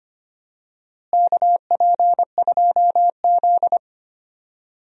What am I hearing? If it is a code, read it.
Morse code, heard as a keyed tone, about 25 wpm, 700 Hz. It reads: KP2Z
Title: DXCC-100 Call Signs - 4 Characters - Repeated 3x - 25wpm